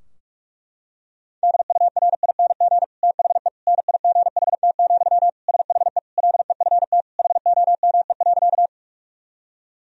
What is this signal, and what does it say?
Morse code, heard as a keyed tone, about 46 wpm, 700 Hz. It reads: DURING THE NIGHT, SHE LEFT HOME.
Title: Sentences from Top 1000 Words - Review Part 2 - 45wpm